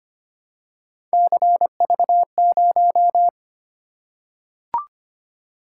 Morse code, heard as a keyed tone, about 25 wpm, 700 Hz. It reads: CV0
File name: Three Letter-Number Combinations - 25wpm